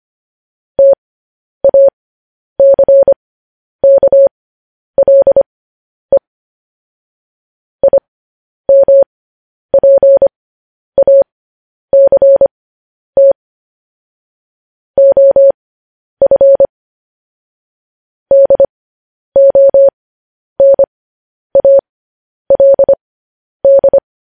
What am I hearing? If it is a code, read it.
Morse code, heard as a keyed tone, about 25 wpm, spaced as for 10 wpm, 550 Hz. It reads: TACKLE IMPACT OF DONALD
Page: news headlines in morse code at 10 wpm